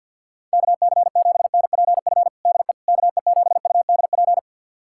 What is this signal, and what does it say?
Morse code, heard as a keyed tone, about 52 wpm, 700 Hz. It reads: KC7NPF DE KE7UDP
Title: USA Callsign DE Callsign - 6 Characters - 50wpm